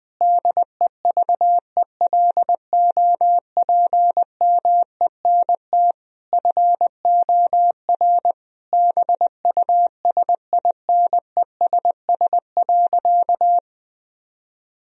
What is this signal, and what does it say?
Morse code, heard as a keyed tone, about 21 wpm, 700 Hz. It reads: DEVELOPMENT FOR BUSINESS.
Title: Sentences from Top 1000 Words - Review Part 4 - 20wpm